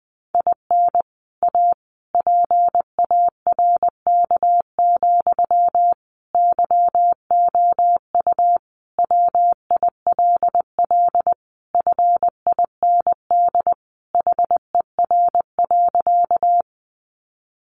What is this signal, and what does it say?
Morse code, heard as a keyed tone, about 20 wpm, 700 Hz. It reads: IN A PARK, YOU WILL FIND HER.
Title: Sentences from Top 1300 Words 20wpm